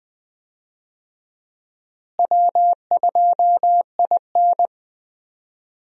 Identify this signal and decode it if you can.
Morse code, heard as a keyed tone, about 20 wpm, 700 Hz. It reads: W2IN
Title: USA Call Signs - 4 Characters - Repeated 3x - 20wpm